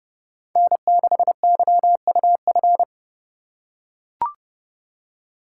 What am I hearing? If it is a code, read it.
Morse code, heard as a keyed tone, about 29 wpm, 700 Hz. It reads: N6YUF